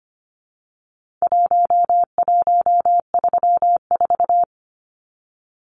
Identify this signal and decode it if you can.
Morse code, heard as a keyed tone, about 25 wpm, 700 Hz. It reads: 1134
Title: Numbers - Contest Style - 4-Digit 25wpm